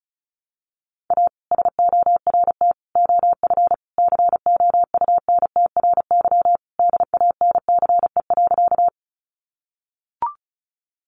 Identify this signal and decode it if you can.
Morse code, heard as a keyed tone, about 34 wpm, 700 Hz. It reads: A SORT OF COUNTRY DANCE.
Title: Sentences from Top 1300 Words 35wpm